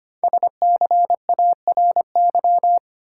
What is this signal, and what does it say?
Morse code, heard as a keyed tone, about 25 wpm, 700 Hz. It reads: SCARY
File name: 5 Letter Words - First 3 Letters Given - 25wpm